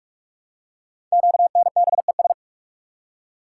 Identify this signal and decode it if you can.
Morse code, heard as a keyed tone, about 44 wpm, 700 Hz. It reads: QNBES
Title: Code Groups - 5 Letters - 45wpm